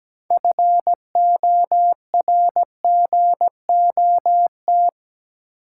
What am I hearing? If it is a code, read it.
Morse code, heard as a keyed tone, about 17 wpm, 700 Hz. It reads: FORGOT
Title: Sentences from Top 1400 Words - New Word Drill Alt - 17wpm